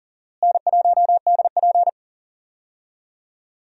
Morse code, heard as a keyed tone, about 40 wpm, 700 Hz. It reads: N1DP
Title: USA Call Signs - 4 Characters - Repeated 2x - 40wpm